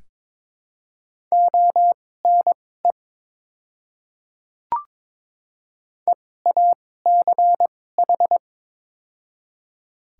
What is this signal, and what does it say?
Morse code, heard as a keyed tone, about 21 wpm, spaced as for 16 wpm, 700 Hz. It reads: ONE EACH
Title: 2x Character Spacing - Top 100 Words - 22wpm